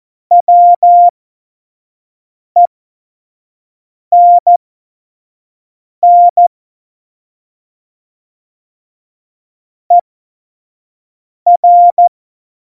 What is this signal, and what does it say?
Morse code, heard as a keyed tone, about 14 wpm, spaced as for 5 wpm, 700 Hz. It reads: WENN ER
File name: Morsekurs / Witze / Witz 21 April 2020, 25 F70